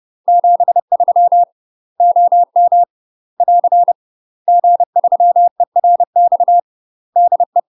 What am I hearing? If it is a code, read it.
Morse code, heard as a keyed tone, about 30 wpm, 700 Hz. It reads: 73 OM <AR> G3ERX DE
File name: UK MASTER QSOs TRAINING / 30WPM / UK MASTER QSOs 0047